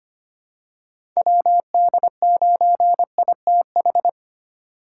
Morse code, heard as a keyed tone, about 25 wpm, 700 Hz. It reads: WD9ITH
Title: USA Call Signs - 8x Character Spacing - 25wpm